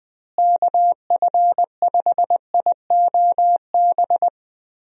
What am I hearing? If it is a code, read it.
Morse code, heard as a keyed tone, about 21 wpm, 700 Hz. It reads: KF5IOB
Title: USA Call Signs - 6 Characters - Repeated 2x - 20wpm